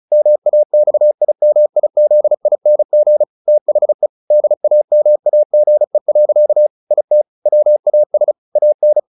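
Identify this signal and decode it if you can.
Morse code, heard as a keyed tone, about 34 wpm, 600 Hz. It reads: MAXIMIZING THE DAMAGE. IT WAS AN
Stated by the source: Morse practice at 35 wpm from CW med Gnister, sending Ken Follett's novel Never in English.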